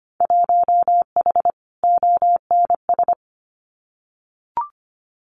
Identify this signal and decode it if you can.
Morse code, heard as a keyed tone, about 25 wpm, 700 Hz. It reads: 1H ONS
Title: Winter Field Day Exchanges 25wpm